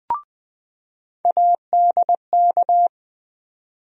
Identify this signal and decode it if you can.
Morse code, heard as a keyed tone, about 20 wpm, 700 Hz. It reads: ADK
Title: USA Call Sign Postfixes - 3 Letters 20wpm